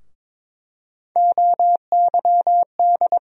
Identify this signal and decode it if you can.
Morse code, heard as a keyed tone, about 22 wpm, 700 Hz. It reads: OYD
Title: USA Call Sign Postfixes - 3 Letters 22wpm